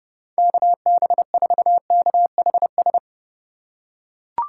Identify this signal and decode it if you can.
Morse code, heard as a keyed tone, about 30 wpm, 700 Hz. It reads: KB4KHS